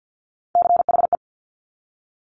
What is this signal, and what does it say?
Morse code, heard as a keyed tone, about 50 wpm, 700 Hz. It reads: CHE
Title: Top Three Letter Combinations in Words 50wpm